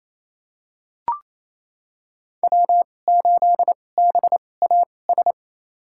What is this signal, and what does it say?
Morse code, heard as a keyed tone, about 27 wpm, spaced as for 20 wpm, 700 Hz. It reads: W8BAS
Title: USA Call Signs - 5 Characters - 2x Character Spacing - 28wpm